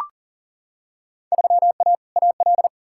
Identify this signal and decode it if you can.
Morse code, heard as a keyed tone, about 40 wpm, 700 Hz. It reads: 3A AL
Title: Field Day Exchanges - 40wpm